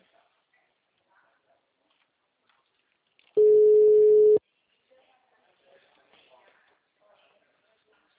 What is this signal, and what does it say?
Ringback tone (Europe, 425 Hz), with speech.
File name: out- -2006- - - .18577